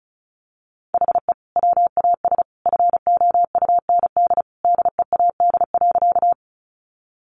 Morse code, heard as a keyed tone, about 34 wpm, 700 Hz. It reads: HE WAS FOUND DEAD.
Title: Sentences from Top 1500 Words 35wpm